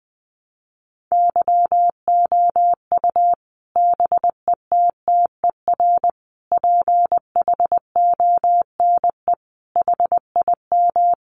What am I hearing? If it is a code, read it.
Morse code, heard as a keyed tone, about 20 wpm, 700 Hz. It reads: YOU BETTER PHONE HIM
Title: Sentences from Top 1100 Words 20wpm